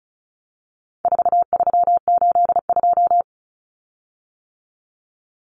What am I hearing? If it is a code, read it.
Morse code, heard as a keyed tone, about 34 wpm, 700 Hz. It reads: 4382
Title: Common Cut Numbers - 4-Digit 35wpm